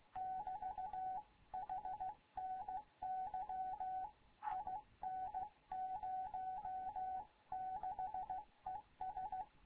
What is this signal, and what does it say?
Morse code, heard as a keyed tone, about 16 wpm, 750 Hz. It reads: <BT> HNY IN06ES
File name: Morze-HA2MN / hangp / 2 11 GYAKORLAT